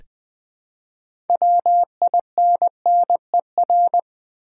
Morse code, heard as a keyed tone, about 20 wpm, 700 Hz. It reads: WINNER
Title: Build A Word - 2 Letters At A Time - 6-Letter Words - 20wpm